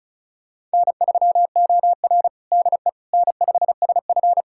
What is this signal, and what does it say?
Morse code, heard as a keyed tone, about 34 wpm, 700 Hz. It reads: N3OR DE N5SF
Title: USA Callsign DE Callsign - 4 Characters - 35wpm